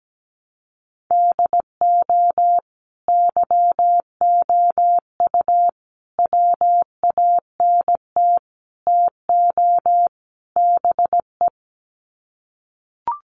Morse code, heard as a keyed tone, about 17 wpm, 700 Hz. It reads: DO YOU WANT TO BE
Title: Sets of 5 Words Encore2 17wpm